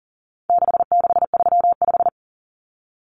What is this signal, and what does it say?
Morse code, heard as a keyed tone, about 40 wpm, 700 Hz. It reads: 6635
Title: Numbers - 4-Digit 40wpm